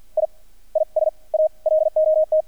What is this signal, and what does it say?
Morse code, heard as a keyed tone, about 48 wpm, 620 Hz. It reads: I I S A LOT
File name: archived / CRSnet